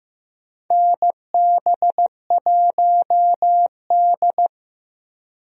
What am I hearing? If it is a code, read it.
Morse code, heard as a keyed tone, about 16 wpm, 700 Hz. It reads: NB1D
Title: USA Call Signs - 4 Characters - Repeated 3x - 15wpm